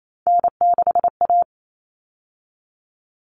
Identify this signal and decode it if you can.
Morse code, heard as a keyed tone, about 28 wpm, 700 Hz. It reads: N6A
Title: All Cut Numbers - 3-Digit 28wpm